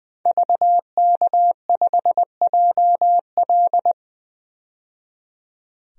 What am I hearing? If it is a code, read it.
Morse code, heard as a keyed tone, about 20 wpm, 700 Hz. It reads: VK5JL
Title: DXCC-100 Call Signs - 5 Characters - 20wpm